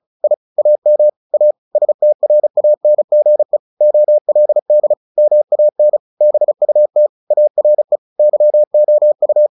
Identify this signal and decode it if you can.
Morse code, heard as a keyed tone, about 34 wpm, 600 Hz. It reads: I AM A STRANGE OLD MAN BUT ARE YOU
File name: the old man and the sea / the old man and the sea 1 35wpm